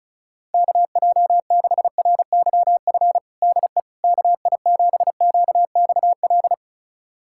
Morse code, heard as a keyed tone, about 36 wpm, 700 Hz. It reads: KJ6RYF DE KI7QXL
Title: USA Callsign DE Callsign - 6 Characters - 35wpm